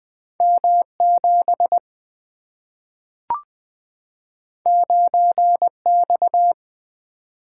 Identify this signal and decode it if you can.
Morse code, heard as a keyed tone, about 20 wpm, 700 Hz. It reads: M 7 9 X
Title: Two Letter-Number Combinations - 20wpm